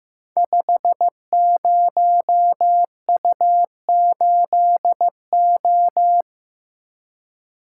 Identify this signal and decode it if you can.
Morse code, heard as a keyed tone, about 15 wpm, 700 Hz. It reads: 50U8O
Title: Code Groups - 5 Letter:Numbers - 15wpm